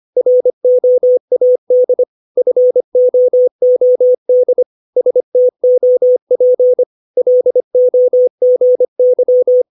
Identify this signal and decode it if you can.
Morse code, heard as a keyed tone, about 25 wpm, 500 Hz. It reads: ROAD FOOD STOP LOGY
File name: 632 words / 632 Words 25WPM